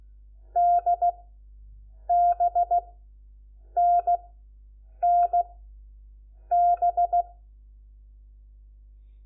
Morse code, heard as a keyed tone, about 16 wpm, 700 Hz. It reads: D B N N B
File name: Lecciones / CLASE Nº4